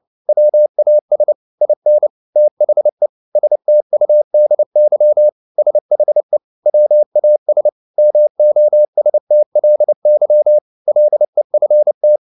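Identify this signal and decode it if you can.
Morse code, heard as a keyed tone, about 29 wpm, 600 Hz. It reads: WAS IN THE STUDY SHE WAS MOSTLY LEFT
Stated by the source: Morse code at 29 wpm, sending Ken Follett's novel Never in English, from CW med Gnister.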